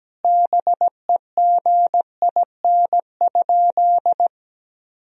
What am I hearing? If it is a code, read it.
Morse code, heard as a keyed tone, about 17 wpm, 700 Hz. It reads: BEGIN?
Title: Sentences from Top 300 Words and Abbreviations 17wpm